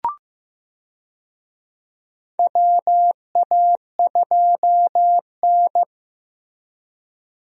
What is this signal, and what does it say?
Morse code, heard as a keyed tone, about 15 wpm, 700 Hz. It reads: WA2N
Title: USA Call Signs - 4 Characters - Repeated 2x - 15wpm